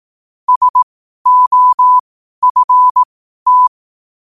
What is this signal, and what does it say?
Morse code, heard as a keyed tone, about 18 wpm, spaced as for 12 wpm, 1000 Hz. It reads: SOFT
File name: qotdfiles / 12wpm